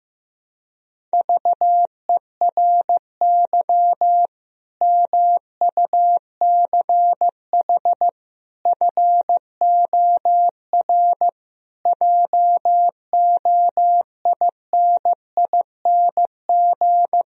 Morse code, heard as a keyed tone, about 15 wpm, 700 Hz. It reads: VERY MUCH FOR JOINING
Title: Speed-Racing - Sets of 5 Words 15wpm